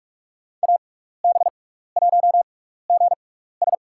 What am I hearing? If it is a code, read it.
Morse code, heard as a keyed tone, about 46 wpm, spaced as for 16 wpm, 700 Hz. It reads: AB1GS
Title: USA Call Signs - 5 Characters - 6x Character Spacing - 45wpm